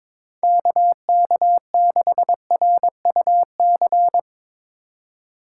Morse code, heard as a keyed tone, about 24 wpm, 700 Hz. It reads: KK6RUC
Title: Speed Racing - USA Call Signs 22wpm